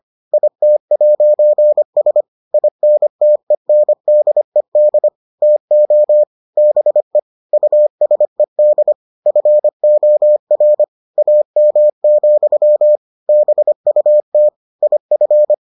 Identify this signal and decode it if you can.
Morse code, heard as a keyed tone, about 25 wpm, 600 Hz. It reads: IT'S INTENDED TO BE USED FOR AM, BUT IF